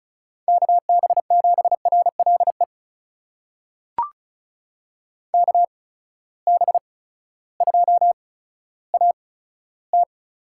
Morse code, heard as a keyed tone, about 36 wpm, spaced as for 16 wpm, 700 Hz. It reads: KB7RLE KB2AT
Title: USA Call Signs - 6 Characters - 8x Character Spacing - 35wpm